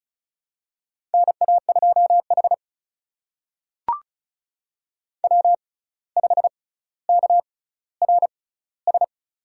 Morse code, heard as a keyed tone, about 36 wpm, spaced as for 17 wpm, 700 Hz. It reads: NA2H W5KRS
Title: USA Call Signs - 6x Character Spacing - 35wpm